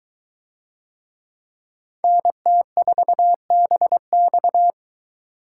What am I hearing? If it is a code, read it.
Morse code, heard as a keyed tone, about 22 wpm, 700 Hz. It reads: NT4BX